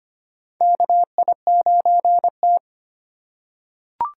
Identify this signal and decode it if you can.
Morse code, heard as a keyed tone, about 25 wpm, 700 Hz. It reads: KI9T
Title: USA Call Signs - 4 Characters - 2x Character Spacing - 25wpm